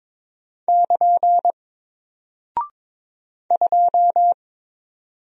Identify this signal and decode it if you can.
Morse code, heard as a keyed tone, about 22 wpm, 700 Hz. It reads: <KN> 2
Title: Single Character (Letter:Number:Punctuation:Prosign) 22wpm